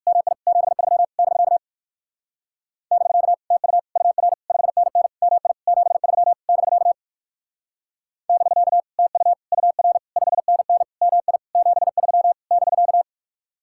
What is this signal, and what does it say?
Morse code, heard as a keyed tone, about 58 wpm, 700 Hz. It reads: MI 73 <BK> <BK> TU UR 5NN MI 73 <BK> <BK> TU UR 5NN MI 73 <BK>